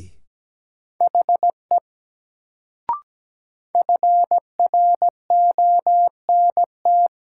Morse code, heard as a keyed tone, about 17 wpm, 700 Hz. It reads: HE FRONT
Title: Sets of 2 Words - New Word Drill - 17wpm